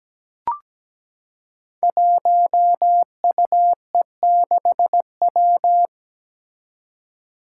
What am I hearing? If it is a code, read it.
Morse code, heard as a keyed tone, about 17 wpm, 700 Hz. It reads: 1UE6W
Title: Five Challenging Letter-Number Combinations 17wpm